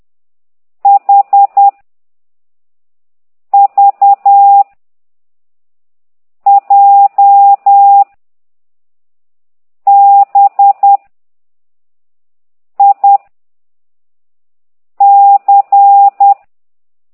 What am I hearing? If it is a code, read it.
Morse code, heard as a keyed tone, about 10 wpm, 800 Hz. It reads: H V J B I C
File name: exerc7